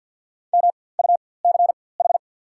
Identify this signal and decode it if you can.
Morse code, heard as a keyed tone, about 54 wpm, spaced as for 24 wpm, 700 Hz. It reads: MUCH
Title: 4x Character Spacing - Top 300 Words - 50wpm